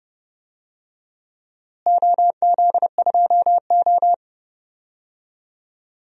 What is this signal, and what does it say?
Morse code, heard as a keyed tone, about 30 wpm, 700 Hz. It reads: OZ2O
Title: DXCC-100 Call Signs - 4 Characters - Repeated 2x - 30wpm